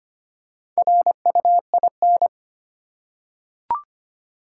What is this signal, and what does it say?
Morse code, heard as a keyed tone, about 25 wpm, 700 Hz. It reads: RUIN
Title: Four Letter Words - 25wpm